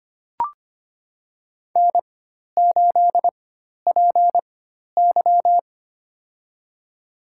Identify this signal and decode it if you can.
Morse code, heard as a keyed tone, about 25 wpm, spaced as for 12 wpm, 700 Hz. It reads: N8PY
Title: USA Call Signs - 4 Characters - 4x Character Spacing - 25wpm